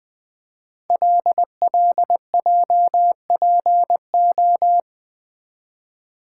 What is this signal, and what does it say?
Morse code, heard as a keyed tone, about 20 wpm, 700 Hz. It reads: LLJPO
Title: Code Groups - 5 Letters - 20wpm